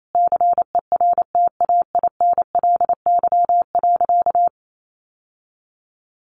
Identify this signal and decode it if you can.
Morse code, heard as a keyed tone, about 28 wpm, 700 Hz. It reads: CERTAINLY.
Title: Sentences from Top 900 Words 28wpm